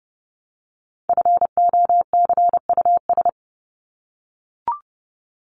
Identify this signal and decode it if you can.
Morse code, heard as a keyed tone, about 30 wpm, 700 Hz. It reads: FOCUS
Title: Top 501-750 Words 30wpm